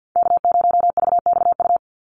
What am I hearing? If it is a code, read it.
Morse code, heard as a keyed tone, about 50 wpm, 700 Hz. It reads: K0VXU
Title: Call Signs Repeated 3x - 50wpm